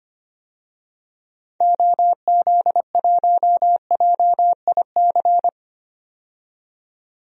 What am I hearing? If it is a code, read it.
Morse code, heard as a keyed tone, about 25 wpm, 700 Hz. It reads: OZ1JIC
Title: DXCC-100 Call Signs - 6 Characters - 25wpm